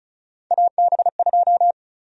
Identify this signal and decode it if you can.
Morse code, heard as a keyed tone, about 36 wpm, 700 Hz. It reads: AB2
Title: USA Call Sign Prefixes - 35wpm